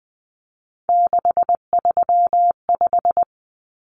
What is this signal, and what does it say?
Morse code, heard as a keyed tone, about 20 wpm, 700 Hz. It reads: 635
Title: Common Cut Numbers - 3-Digit 20wpm